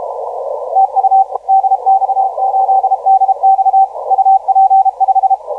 Morse code, heard as a keyed tone, about 32 wpm, 770 Hz. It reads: TU DX8DX UP5
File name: DX8DX 1904UTC 20m